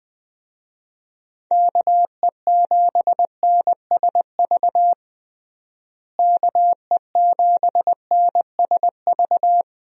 Morse code, heard as a keyed tone, about 20 wpm, 700 Hz. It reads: KE7NSV KE7NSV
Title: USA Call Signs - 6 Characters - Repeated 2x - 20wpm